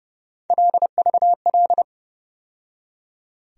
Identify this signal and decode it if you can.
Morse code, heard as a keyed tone, about 32 wpm, 700 Hz. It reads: LVL
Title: USA Call Sign Postfixes - 3 Letters 30wpm